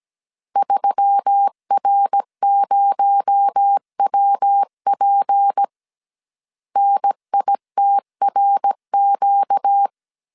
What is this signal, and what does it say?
Morse code, heard as a keyed tone, about 17 wpm, 800 Hz. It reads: 3R0WP NITRQ
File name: CW / DET / 1 6 1500C S A 16wpm 2635